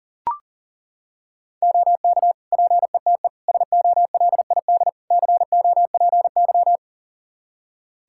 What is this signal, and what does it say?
Morse code, heard as a keyed tone, about 40 wpm, 700 Hz. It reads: OK PETE SOLID COPY
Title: Practice QSO - Exchange 3 - More Info - 40wpm